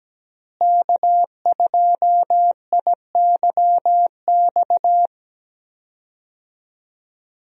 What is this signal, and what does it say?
Morse code, heard as a keyed tone, about 17 wpm, 700 Hz. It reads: K2IYX
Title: USA Call Signs - 5 Characters - Repeated 2x - 17wpm